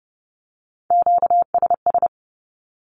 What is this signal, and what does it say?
Morse code, heard as a keyed tone, about 30 wpm, 700 Hz. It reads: QSS
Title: Q-Codes with Spoken Meaning - 30wpm